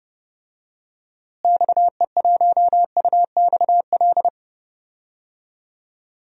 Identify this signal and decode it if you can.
Morse code, heard as a keyed tone, about 30 wpm, 700 Hz. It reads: XE1UXL
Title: DXCC-100 Call Signs - 6 Characters - Repeated 2x - 30wpm